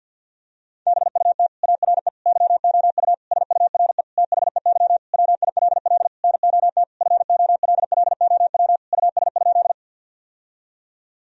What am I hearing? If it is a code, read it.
Morse code, heard as a keyed tone, about 50 wpm, 700 Hz. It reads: BUT ARE YOU SURE THEY WILL NOT FOLLOW US?
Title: Sentences from Top 1000 Words - Review Part 1 - 50wpm